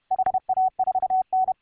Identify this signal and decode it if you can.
Morse code, heard as a keyed tone, about 32 wpm, 750 Hz. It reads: HA4N